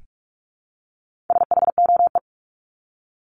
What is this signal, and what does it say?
Morse code, heard as a keyed tone, about 44 wpm, 700 Hz. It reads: SHOE
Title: Top 1751-2000 Words 45wpm